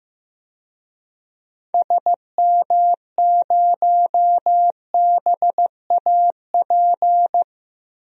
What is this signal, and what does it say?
Morse code, heard as a keyed tone, about 15 wpm, 700 Hz. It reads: SM0BAP
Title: DXCC-100 Call Signs - 6 Characters - 15wpm